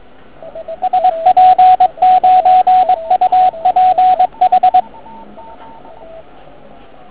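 Morse code, heard as a keyed tone, about 22 wpm, 750 Hz, with radio noise. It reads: SP9UPH
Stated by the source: Morning 80m pile-up recorded at nearly the same time on three transceivers: FT1000MKV, FT817 and ATS-3B.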